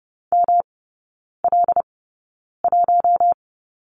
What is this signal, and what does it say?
Morse code, heard as a keyed tone, about 30 wpm, spaced as for 9 wpm, 700 Hz. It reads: ML1
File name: Lesson 18e - Letter m - Three Characters 30wpm